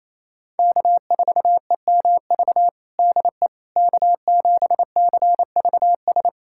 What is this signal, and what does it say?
Morse code, heard as a keyed tone, about 27 wpm, 700 Hz. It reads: K4EMV DE K7CVS
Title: USA Callsign DE Callsign - 5 Characters - 28wpm